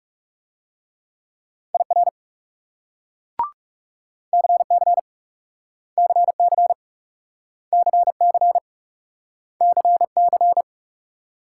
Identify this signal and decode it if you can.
Morse code, heard as a keyed tone, about 38 wpm, 700 Hz. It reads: IR CC CC CC CC
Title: Speed-Racing Multi-letter Phonemes - 30wpm